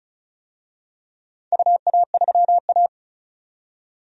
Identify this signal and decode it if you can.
Morse code, heard as a keyed tone, about 36 wpm, 700 Hz. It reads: UA3A
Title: DXCC-100 Call Signs - 4 Characters - 35wpm